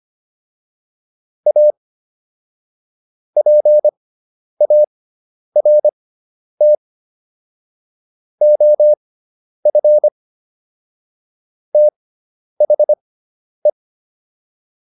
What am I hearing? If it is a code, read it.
Morse code, heard as a keyed tone, about 25 wpm, spaced as for 10 wpm, 600 Hz. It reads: A PART OF THE